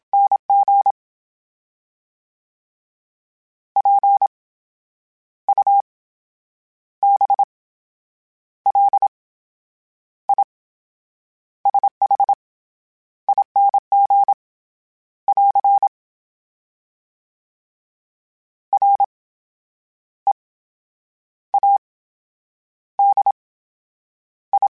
Morse code, heard as a keyed tone, about 27 wpm, spaced as for 9 wpm, 800 Hz. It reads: NG PUBLISHING <AR> READI
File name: Special INGs / Special INGs-mp3s / VerySlowToFaster Ing @25